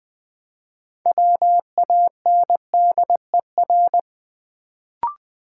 Morse code, heard as a keyed tone, about 20 wpm, 700 Hz. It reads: WANDER
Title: Build A Word - 2 Letters At A Time - 6-Letter Words - 20wpm